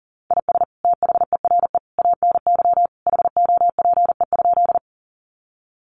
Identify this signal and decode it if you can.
Morse code, heard as a keyed tone, about 40 wpm, 700 Hz. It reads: IS THERE ANY HOPE?
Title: Sentences from Top 1000 Words 40wpm